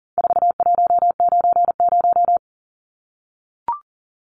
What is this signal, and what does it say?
Morse code, heard as a keyed tone, about 39 wpm, 700 Hz. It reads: 4190
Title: Numbers - 4-Digit 40wpm